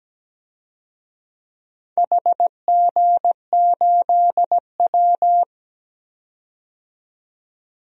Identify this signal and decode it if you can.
Morse code, heard as a keyed tone, about 17 wpm, 700 Hz. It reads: HG8W
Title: DXCC-100 Call Signs - 4 Characters - 17wpm